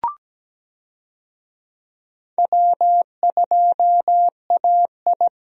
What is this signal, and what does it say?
Morse code, heard as a keyed tone, about 17 wpm, 700 Hz. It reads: W2AI
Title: USA Call Signs - 4 Characters - Repeated 3x - 17wpm